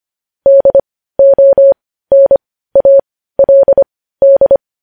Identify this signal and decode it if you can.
Morse code, heard as a keyed tone, about 25 wpm, spaced as for 15 wpm, 550 Hz. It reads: DONALD
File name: podcast / News headlines-1x-25x15